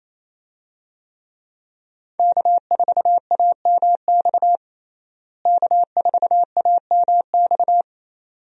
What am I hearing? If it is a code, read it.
Morse code, heard as a keyed tone, about 27 wpm, 700 Hz. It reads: K4AMX K4AMX
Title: USA Call Signs - 5 Characters - Repeated 2x - 28wpm